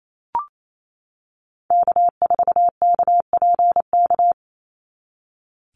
Morse code, heard as a keyed tone, about 28 wpm, 700 Hz. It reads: K4KPK
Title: NA SOTA Activator Callsigns 28wpm